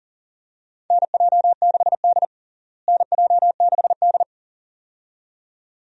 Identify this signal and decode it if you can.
Morse code, heard as a keyed tone, about 40 wpm, 700 Hz. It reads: NJ6D NJ6D
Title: USA Call Signs - 4 Characters - Repeated 2x - 40wpm